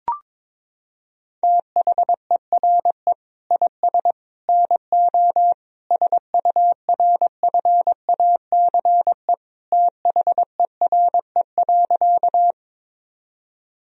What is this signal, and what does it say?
Morse code, heard as a keyed tone, about 22 wpm, 700 Hz. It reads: THERE IS NO SURFACE THERE.
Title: Sentences from Top 1000 Words - Review Part 3 - 22wpm